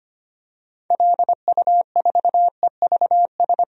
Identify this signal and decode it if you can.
Morse code, heard as a keyed tone, about 25 wpm, 700 Hz. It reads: LU4EVS
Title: DXCC-100 Call Signs - 6 Characters - 25wpm